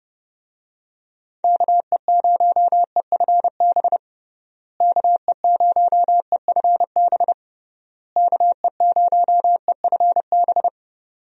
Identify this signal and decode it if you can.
Morse code, heard as a keyed tone, about 30 wpm, 700 Hz. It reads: KE0EFB KE0EFB KE0EFB
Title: USA Call Signs - 6 Characters - Repeated 3x - 30wpm